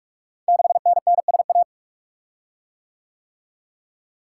Morse code, heard as a keyed tone, about 44 wpm, 690 Hz. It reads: 6NNSA
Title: Code Groups - 5 Letter:Numbers - 45wpm